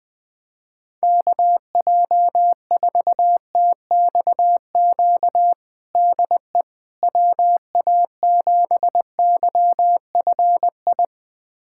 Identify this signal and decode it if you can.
Morse code, heard as a keyed tone, about 20 wpm, 700 Hz. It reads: KJ4TXQ DE WA7YFI
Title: USA Callsign DE Callsign - 6 Characters - 20wpm